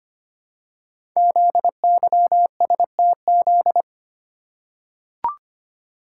Morse code, heard as a keyed tone, about 25 wpm, 700 Hz. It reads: ZYSTZ
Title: Code Groups - 5 Letters - 25wpm